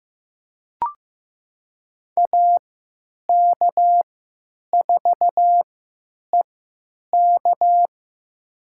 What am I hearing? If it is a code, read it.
Morse code, heard as a keyed tone, about 15 wpm, spaced as for 8 wpm, 700 Hz. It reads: AK4EK
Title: USA Call Signs - 5 Characters - 3x Character Spacing - 15wpm